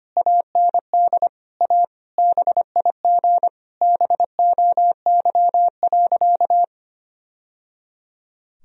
Morse code, heard as a keyed tone, about 25 wpm, 700 Hz. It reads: AND A BIG BOY.
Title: Sentences from Top 1000 Words - Review Part 1 - 25wpm